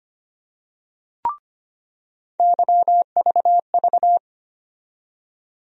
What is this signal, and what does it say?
Morse code, heard as a keyed tone, about 25 wpm, 700 Hz. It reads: YVV
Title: USA Call Sign Postfixes - 3 Letters 25wpm